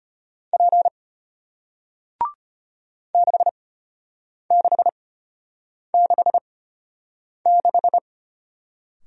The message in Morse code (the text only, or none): P 6666